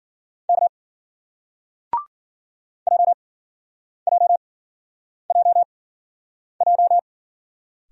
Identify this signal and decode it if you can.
Morse code, heard as a keyed tone, about 51 wpm, spaced as for 9 wpm, 700 Hz. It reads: K JJJJ